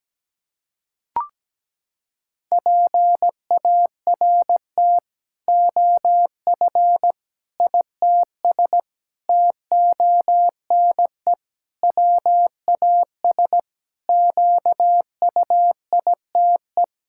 Morse code, heard as a keyed tone, about 17 wpm, 700 Hz. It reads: PART OF ITS TONE WAS QUITE
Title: Sentences from Top 1600 Words 17wpm